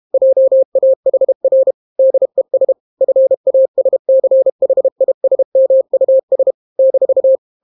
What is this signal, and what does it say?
Morse code, heard as a keyed tone, about 32 wpm, 530 Hz. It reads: JAHR DES FASCHISMUS -